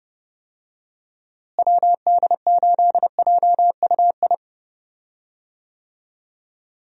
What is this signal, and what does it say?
Morse code, heard as a keyed tone, about 30 wpm, 700 Hz. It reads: WD8JUI
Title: USA Call Signs - 6 Characters - Repeated 3x - 30wpm